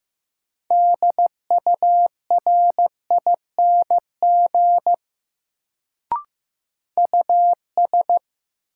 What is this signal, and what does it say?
Morse code, heard as a keyed tone, about 15 wpm, 700 Hz. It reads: DURING US
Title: Sets of 2 Words - New Word Drill - 15wpm